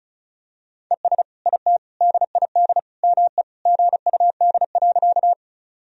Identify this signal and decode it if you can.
Morse code, heard as a keyed tone, about 36 wpm, 700 Hz. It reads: ES IT DID ME GUD.
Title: Sentences from Top 200 Words and Abbreviations 35wpm